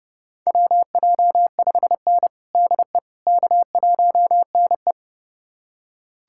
Morse code, heard as a keyed tone, about 30 wpm, 700 Hz. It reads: WJ5N DE K1NE
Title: USA Callsign DE Callsign - 4 Characters - 30wpm